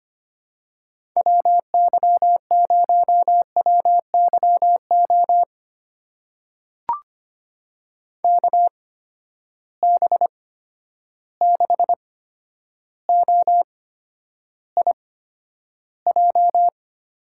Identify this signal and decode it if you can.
Morse code, heard as a keyed tone, about 25 wpm, spaced as for 11 wpm, 700 Hz. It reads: WY0WYO KB6OIJ
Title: USA Call Signs - 6 Characters - 8x Character Spacing - 25wpm